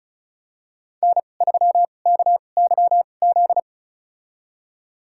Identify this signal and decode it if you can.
Morse code, heard as a keyed tone, about 34 wpm, spaced as for 25 wpm, 700 Hz. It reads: N3KYZ
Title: USA Call Signs - 2x Character Spacing - 35wpm